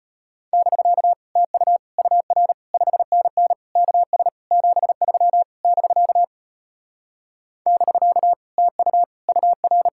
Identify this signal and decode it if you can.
Morse code, heard as a keyed tone, about 33 wpm, 700 Hz. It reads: <BK> TU UR 5NN KS 73 <BK> <BK> TU UR